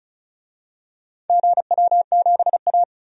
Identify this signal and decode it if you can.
Morse code, heard as a keyed tone, about 36 wpm, 700 Hz. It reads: GW7A
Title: DXCC-100 Call Signs - 4 Characters - 35wpm